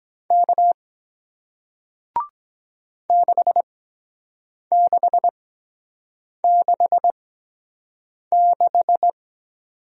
Morse code, heard as a keyed tone, about 23 wpm, spaced as for 7 wpm, 700 Hz. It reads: K 6666